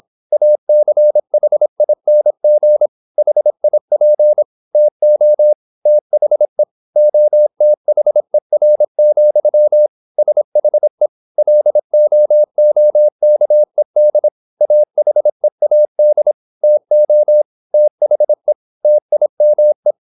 Morse code, heard as a keyed tone, about 26 wpm, 600 Hz. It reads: ACHING HIP TO THE OTHER, SHE LOOKED AHEAD TO THE TIME
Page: Never 26wpm | CW med Gnister